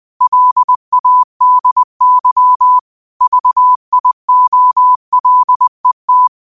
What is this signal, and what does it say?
Morse code, heard as a keyed tone, about 20 wpm, 1000 Hz. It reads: LADY VIOLET
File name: qotdfiles / 20wpm